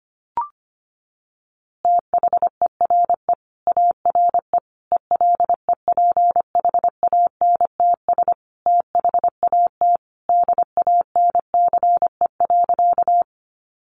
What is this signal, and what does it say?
Morse code, heard as a keyed tone, about 25 wpm, 700 Hz. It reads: THERE ARE ELEPHANTS THAT DANCE.
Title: Easy Phrases 25wpm